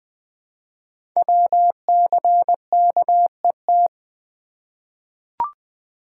Morse code, heard as a keyed tone, about 20 wpm, 700 Hz. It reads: WCKET
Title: Code Groups - 5 Letters - 20wpm